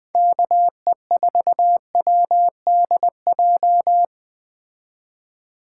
Morse code, heard as a keyed tone, about 20 wpm, 700 Hz. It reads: KE4WDJ